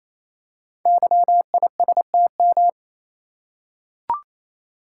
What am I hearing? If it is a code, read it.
Morse code, heard as a keyed tone, about 27 wpm, 700 Hz. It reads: YISTM
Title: Code Groups - 5 Letters - 28wpm